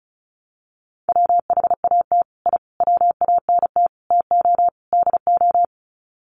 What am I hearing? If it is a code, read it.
Morse code, heard as a keyed tone, about 34 wpm, 700 Hz. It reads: WHAT I WANT TO DO
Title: Sets of 5 Words Encore 35wpm